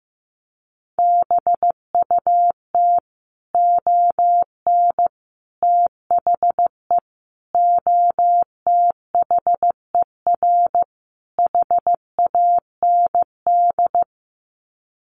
Morse code, heard as a keyed tone, about 15 wpm, 700 Hz. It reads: BUT ON THE OTHER HAND
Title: Sets of 5 Words 15wpm